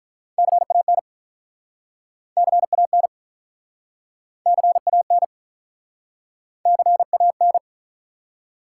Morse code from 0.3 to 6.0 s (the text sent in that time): CAN CAN CAN